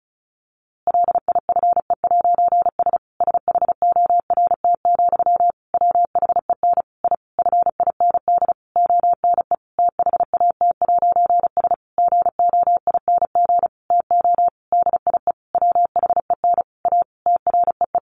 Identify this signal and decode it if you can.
Morse code, heard as a keyed tone, about 34 wpm, 700 Hz. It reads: LIFE'S SHORT, WHEN I FIND ONE THAT'S GOING TO DIE WHEN A TREE
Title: A Sound Of Thunder 35wpm